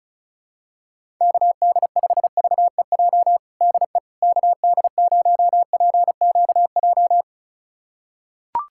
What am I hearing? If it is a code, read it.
Morse code, heard as a keyed tone, about 36 wpm, 700 Hz. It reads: KD5VEJ DE KD0PQJ
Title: USA Callsign DE Callsign - 6 Characters - 35wpm